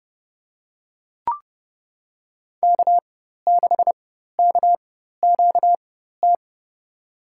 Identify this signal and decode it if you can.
Morse code, heard as a keyed tone, about 31 wpm, spaced as for 14 wpm, 700 Hz. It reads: K6KQT
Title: USA Call Signs - 5 Characters - 4x Character Spacing - 30wpm